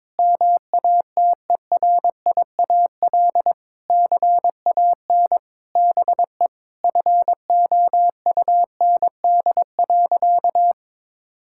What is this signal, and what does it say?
Morse code, heard as a keyed tone, about 22 wpm, 700 Hz. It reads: MATERIAL CAN BE FOUND.
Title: Sentences from Top 1000 Words - Review Part 5 - 22wpm